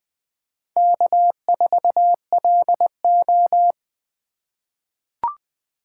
Morse code, heard as a keyed tone, about 20 wpm, 700 Hz. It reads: K4LO
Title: USA Call Signs - 8x Character Spacing - 20wpm